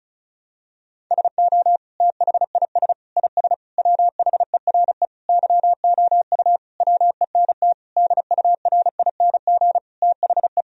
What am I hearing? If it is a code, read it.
Morse code, heard as a keyed tone, about 34 wpm, 700 Hz. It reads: SO THIS IS WHERE YOU WENT DURING THE
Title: Sentences from Top 1000 Words - Review Part 2 - 35wpm